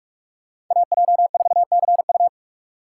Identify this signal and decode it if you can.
Morse code, heard as a keyed tone, about 44 wpm, 700 Hz. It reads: AJ4CU